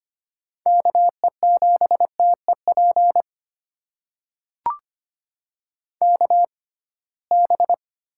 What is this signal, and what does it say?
Morse code, heard as a keyed tone, about 25 wpm, spaced as for 19 wpm, 700 Hz. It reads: KE7TEP KB
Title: USA Call Signs - 6 Characters - 6x Character Spacing - 25wpm